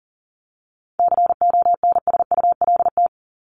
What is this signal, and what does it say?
Morse code, heard as a keyed tone, about 40 wpm, 700 Hz. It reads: CONSULT
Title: Top 1751-2000 Words 40wpm